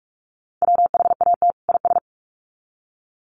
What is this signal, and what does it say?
Morse code, heard as a keyed tone, about 44 wpm, 700 Hz. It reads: WHAT IS
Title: Sets of 2 Words 45wpm